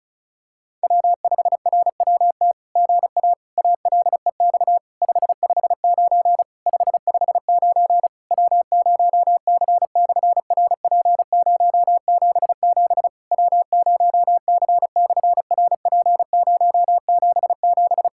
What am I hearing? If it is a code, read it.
Morse code, heard as a keyed tone, about 33 wpm, 700 Hz. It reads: W5RWT GA ALEX 559 559 W0C/RP077 W0C/RP077
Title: SOTA QSO Practice 35wpm